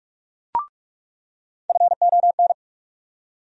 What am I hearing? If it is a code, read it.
Morse code, heard as a keyed tone, about 46 wpm, 700 Hz. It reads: FON
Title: USA Call Sign Postfixes - 3 Letters 45wpm